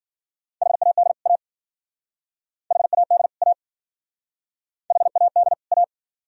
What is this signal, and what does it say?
Morse code, heard as a keyed tone, about 51 wpm, 700 Hz. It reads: HAD A HAD A HAD A